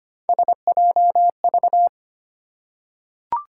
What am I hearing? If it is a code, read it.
Morse code, heard as a keyed tone, about 25 wpm, 700 Hz. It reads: SJV
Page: morse-code-ninja-practice-25wpm